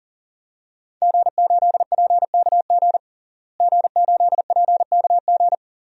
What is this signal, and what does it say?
Morse code, heard as a keyed tone, about 37 wpm, 700 Hz. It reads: G8PKG G8PKG
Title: DXCC-100 Call Signs - 5 Characters - Repeated 3x - 40wpm